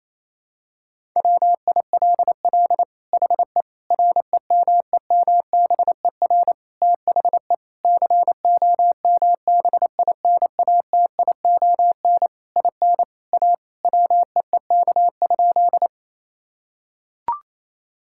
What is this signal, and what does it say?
Morse code, heard as a keyed tone, about 27 wpm, 700 Hz. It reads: WILL HE REMEMBER THE COMBINATION IN A WEEK?
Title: Sentences from Top 1000 Words - Review Part 6 - 28wpm